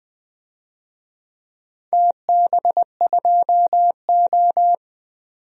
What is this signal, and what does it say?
Morse code, heard as a keyed tone, about 20 wpm, 700 Hz. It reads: TB2O
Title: DXCC-100 Call Signs - 4 Characters - 20wpm